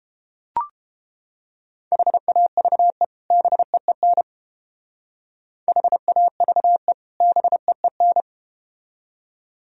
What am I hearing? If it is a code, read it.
Morse code, heard as a keyed tone, about 31 wpm, 700 Hz. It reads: HAVE BEEN HAVE BEEN